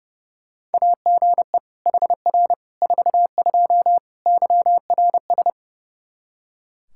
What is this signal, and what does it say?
Morse code, heard as a keyed tone, about 30 wpm, 700 Hz. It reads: AGE HR 42 YRS
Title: Practice QSO - Exchange 3 - More Info - 30wpm